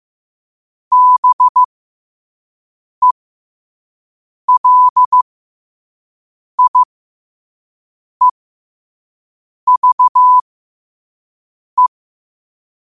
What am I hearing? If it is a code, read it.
Morse code, heard as a keyed tone, about 15 wpm, spaced as for 5 wpm, 1000 Hz. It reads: BELIEVE
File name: qotdfiles / 5wpm